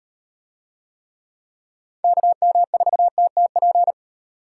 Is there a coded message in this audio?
Yes: Morse code from 2.0 to 3.9 s, heard as a keyed tone, about 38 wpm, 700 Hz. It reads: KM4TTP